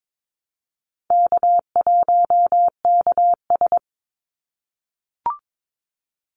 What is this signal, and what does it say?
Morse code, heard as a keyed tone, about 22 wpm, 700 Hz. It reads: K1KS
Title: Easy Call Signs - 22wpm